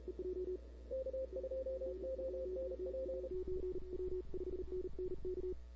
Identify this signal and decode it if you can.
Morse code, heard as a keyed tone, about 32 wpm, 530 Hz, with noise. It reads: K2ONP
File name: CW / 2005-k5tr arrl cw dx -00:00